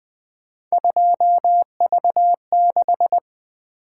Morse code, heard as a keyed tone, about 21 wpm, 700 Hz. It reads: 2V6
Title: Three Letter-Number Combinations - 20wpm